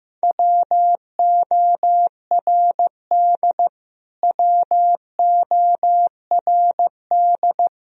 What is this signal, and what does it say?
Morse code, heard as a keyed tone, about 15 wpm, 700 Hz. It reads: WORD WORD
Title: Top 100 Words Repeated 15wpm